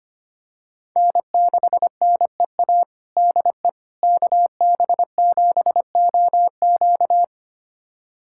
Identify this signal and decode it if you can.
Morse code, heard as a keyed tone, about 25 wpm, 700 Hz. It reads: N6NEA DE KB7OQ
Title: USA Callsign DE Callsign - 5 Characters - 25wpm